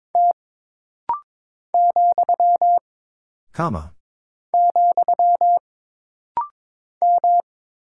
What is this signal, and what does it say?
Morse code, heard as a keyed tone, about 22 wpm, 700 Hz. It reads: T , , M
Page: This rapid-fire timing is geared to push your ICR (Instant Character Recognition) skills.